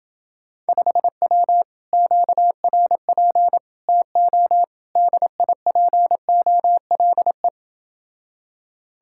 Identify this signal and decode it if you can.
Morse code, heard as a keyed tone, about 28 wpm, 700 Hz. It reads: 5W QRP TO DIPOLE